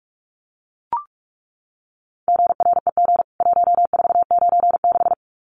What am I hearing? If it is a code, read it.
Morse code, heard as a keyed tone, about 44 wpm, 700 Hz. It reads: GREG 1496
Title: CWT Exchange 45wpm